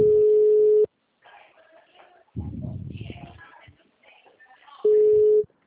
Ringback tone (Europe, 425 Hz).
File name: out- -2004- - - .21067